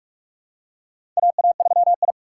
Morse code, heard as a keyed tone, about 44 wpm, 700 Hz. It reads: AA3I
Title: USA Call Signs - 4 Characters - 6x Character Spacing - 45wpm